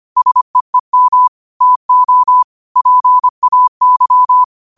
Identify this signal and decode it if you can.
Morse code, heard as a keyed tone, about 25 wpm, 1000 Hz. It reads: SEEM TO PAY